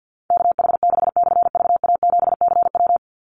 Morse code, heard as a keyed tone, about 48 wpm, 700 Hz. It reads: KH6/VA7CW
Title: Call Signs - 50wpm